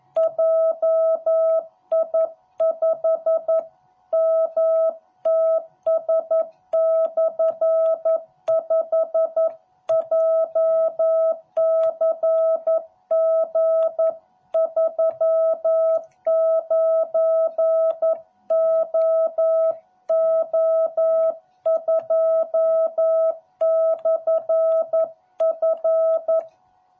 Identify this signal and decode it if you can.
Morse code, heard as a keyed tone, about 11 wpm, 660 Hz. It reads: JI5MTS/5 JCG39OO2/F